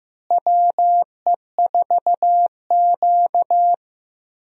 Morse code, heard as a keyed tone, about 15 wpm, 700 Hz. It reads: WE4Q
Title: USA Call Signs - 4 Characters - Repeated 3x - 15wpm